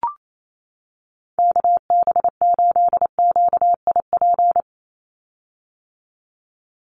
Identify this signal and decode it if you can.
Morse code, heard as a keyed tone, about 28 wpm, 700 Hz. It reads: KB8QIP
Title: USA Call Signs 28wpm